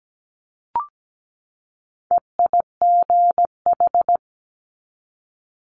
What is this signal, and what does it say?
Morse code, heard as a keyed tone, about 17 wpm, 700 Hz. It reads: EIGH
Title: Multi-letter Phonemes 17wpm